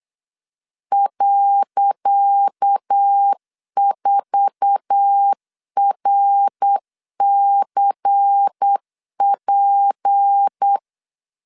Morse code, heard as a keyed tone, about 8 wpm, 800 Hz. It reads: .4RCP
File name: LEZIONI / 10 SIGNS / 10C DOT 8wpm 3649